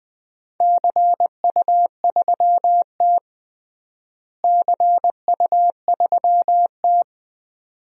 Morse code, heard as a keyed tone, about 20 wpm, 700 Hz. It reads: CU3T CU3T
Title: DXCC-100 Call Signs - 4 Characters - Repeated 2x - 20wpm